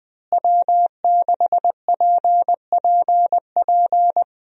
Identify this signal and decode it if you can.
Morse code, heard as a keyed tone, about 20 wpm, 700 Hz. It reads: W6PPP
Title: USA Call Signs - 5 Characters - Repeated 2x - 20wpm